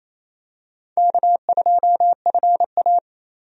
Morse code, heard as a keyed tone, about 28 wpm, 700 Hz. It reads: K2FA
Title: USA Call Signs - 4 Characters - 28wpm